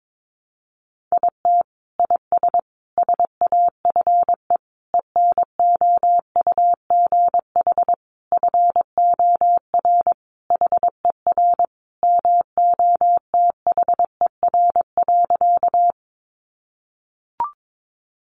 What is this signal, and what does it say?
Morse code, heard as a keyed tone, about 22 wpm, 700 Hz. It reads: IT IS SAFE ENOUGH FOR HER MOTHER.
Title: Sentences from Top 900 Words 22wpm